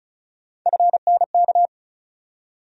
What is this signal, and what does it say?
Morse code, heard as a keyed tone, about 36 wpm, 700 Hz. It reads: FNK
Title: USA Call Sign Postfixes - 3 Letters 35wpm